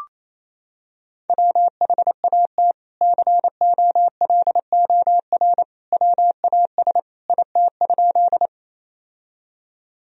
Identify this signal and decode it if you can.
Morse code, heard as a keyed tone, about 28 wpm, 700 Hz. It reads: WHAT COLOR WAS IT?
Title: Sentences from Top 1000 Words - Review Part 2 - 28wpm